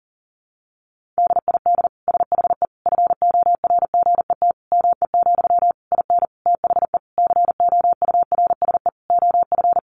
Morse code, heard as a keyed tone, about 37 wpm, 700 Hz. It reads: DID SHE FORGET ME, IN THE COURSE OF
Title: Sentences from Top 1200 Words 40wpm